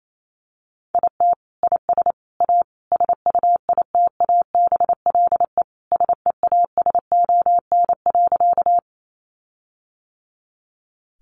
Morse code, heard as a keyed tone, about 28 wpm, 700 Hz. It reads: IT IS A SUITABLE SEASON.
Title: Sentences from Top 1400 Words 28wpm